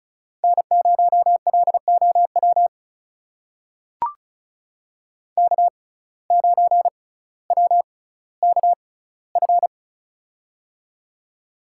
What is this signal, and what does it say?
Morse code, heard as a keyed tone, about 36 wpm, spaced as for 18 wpm, 700 Hz. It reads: N0LOW K9WKF
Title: USA Call Signs - 5 Characters - 6x Character Spacing - 35wpm